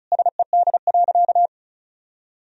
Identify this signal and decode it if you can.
Morse code, heard as a keyed tone, about 36 wpm, 700 Hz. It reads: SED.
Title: Sentences from Top 400 Words and Abbreviations 35wpm